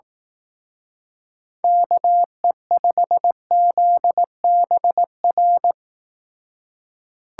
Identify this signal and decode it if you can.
Morse code, heard as a keyed tone, about 18 wpm, 700 Hz. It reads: KE5ZBR
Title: Speed Racing - USA 2x3 Call Signs 15wpm